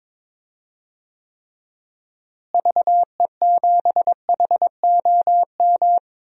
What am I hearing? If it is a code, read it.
Morse code, heard as a keyed tone, about 22 wpm, 700 Hz. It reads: VE7HOM
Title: DXCC-100 Call Signs - 6 Characters - Repeated 3x - 22wpm